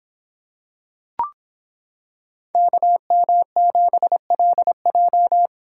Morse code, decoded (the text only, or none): KM7LJ